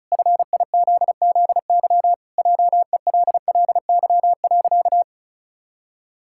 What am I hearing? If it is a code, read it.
Morse code, heard as a keyed tone, about 36 wpm, 700 Hz. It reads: FIZZY JELLY.
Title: Pangram Challenge v2 - 35wpm